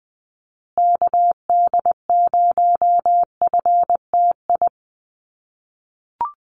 Morse code, heard as a keyed tone, about 20 wpm, 700 Hz. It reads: KD0FTI
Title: USA Call Signs Repeated 3x 20wpm